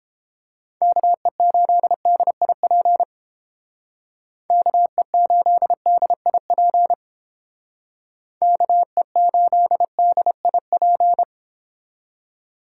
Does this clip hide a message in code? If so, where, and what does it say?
Yes: Morse code from 0.8 to 11.2 s, heard as a keyed tone, about 29 wpm, 700 Hz. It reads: KE8DIP KE8DIP KE8DIP